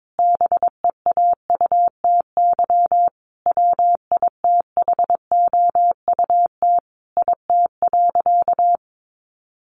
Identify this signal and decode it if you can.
Morse code, heard as a keyed tone, about 22 wpm, 700 Hz. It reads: BEAUTY WITHOUT IT.
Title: Sentences from Top 600 Words 22wpm